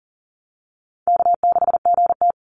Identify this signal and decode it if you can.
Morse code, heard as a keyed tone, about 40 wpm, 700 Hz. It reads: K6GT
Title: Call Signs Repeated 2x - 40wpm